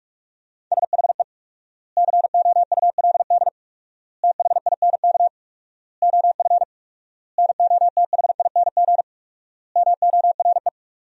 Morse code, heard as a keyed tone, about 46 wpm, 700 Hz. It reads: SHE COULD THINK OF NOTHING MORE
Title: 4x Word-Spacing - Sentences from Top 500 Words - 45wpm